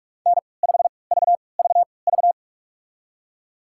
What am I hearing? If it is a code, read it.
Morse code, heard as a keyed tone, about 43 wpm, spaced as for 25 wpm, 700 Hz. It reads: N5VVV
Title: USA Call Signs - 5 Characters - 3x Character Spacing - 45wpm